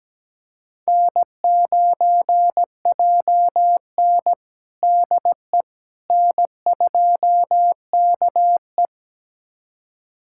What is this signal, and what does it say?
Morse code, heard as a keyed tone, about 17 wpm, 700 Hz. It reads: N9JN DE N2KE
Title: USA Callsign DE Callsign - 4 Characters - 17wpm